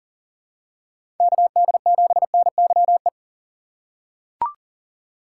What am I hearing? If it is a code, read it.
Morse code, heard as a keyed tone, about 37 wpm, 700 Hz. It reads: KD7NYE
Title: USA Call Signs - 6 Characters - 4x Character Spacing - 40wpm